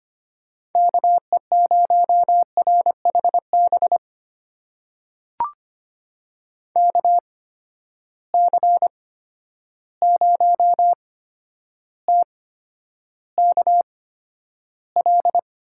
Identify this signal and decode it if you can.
Morse code, heard as a keyed tone, about 25 wpm, spaced as for 11 wpm, 700 Hz. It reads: KE0RHB KC0TKL
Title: USA Call Signs - 6 Characters - 8x Character Spacing - 25wpm